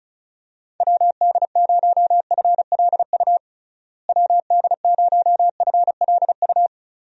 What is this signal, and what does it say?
Morse code, heard as a keyed tone, about 34 wpm, 700 Hz. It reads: WD0FLU WD0FLU
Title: USA Call Signs - 6 Characters - Repeated 3x - 35wpm